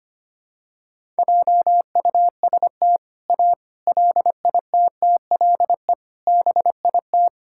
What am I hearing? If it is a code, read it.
Morse code, heard as a keyed tone, about 25 wpm, 700 Hz. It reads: JUST A LITTLE BIT
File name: practice / 25wpm / Sentences with 4-Word Phrases - 25wpm